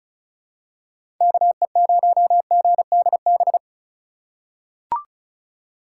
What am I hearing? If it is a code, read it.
Morse code, heard as a keyed tone, about 34 wpm, 700 Hz. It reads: KE0GDB
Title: USA Call Signs - 4x Character Spacing - 35wpm